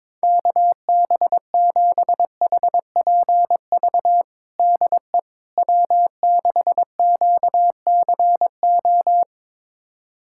Morse code, heard as a keyed tone, about 22 wpm, 700 Hz. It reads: KB7HPV DE W6QCO
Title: USA Callsign DE Callsign 22wpm